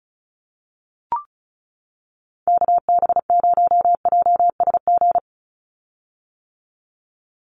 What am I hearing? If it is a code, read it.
Morse code, heard as a keyed tone, about 34 wpm, 700 Hz. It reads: KB0JSG
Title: USA Call Signs 35wpm